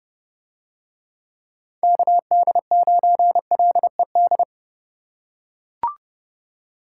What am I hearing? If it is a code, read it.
Morse code, heard as a keyed tone, about 28 wpm, 700 Hz. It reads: KD9LED